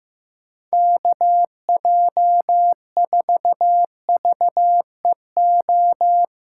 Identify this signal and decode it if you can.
Morse code, heard as a keyed tone, about 15 wpm, 700 Hz. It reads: KJ4VEO
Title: USA Call Signs - 2x Character Spacing - 15wpm